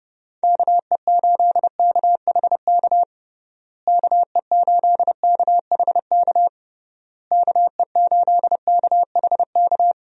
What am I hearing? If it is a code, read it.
Morse code, heard as a keyed tone, about 30 wpm, 700 Hz. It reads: KE8KHK KE8KHK KE8KHK
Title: USA Call Signs - 6 Characters - Repeated 3x - 30wpm